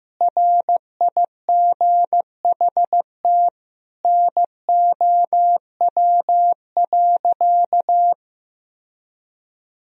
Morse code, heard as a keyed tone, about 15 wpm, 700 Hz. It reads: RIGHT NOW.
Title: Sentences from Top 1000 Words - Review Part 4 - 15wpm